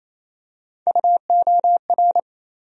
Morse code, heard as a keyed tone, about 27 wpm, 700 Hz. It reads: UOR
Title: USA Call Sign Postfixes - 3 Letters 28wpm